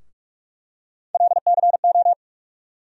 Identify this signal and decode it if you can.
Morse code, heard as a keyed tone, about 46 wpm, 700 Hz. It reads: LCO
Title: USA Call Sign Postfixes - 3 Letters 45wpm